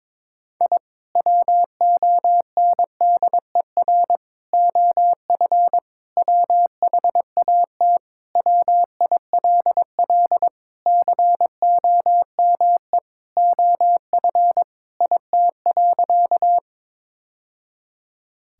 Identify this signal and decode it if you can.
Morse code, heard as a keyed tone, about 22 wpm, 700 Hz. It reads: I WONDER OF WHAT WILL COME OF IT.
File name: practice / 22wpm / Sentences with 2-Word Phrases - 22wpm